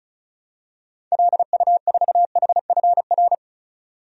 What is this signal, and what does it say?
Morse code, heard as a keyed tone, about 34 wpm, 700 Hz. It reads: LU4HFR
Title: DXCC-100 Call Signs - 6 Characters - 35wpm